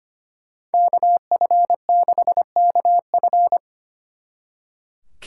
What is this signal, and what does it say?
Morse code, heard as a keyed tone, about 33 wpm, 700 Hz. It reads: KF6KF